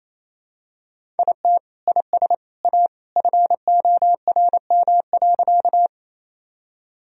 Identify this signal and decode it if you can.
Morse code, heard as a keyed tone, about 28 wpm, 700 Hz. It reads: IT IS A FORM.
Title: Sentences from Top 1000 Words - Review Part 1 - 28wpm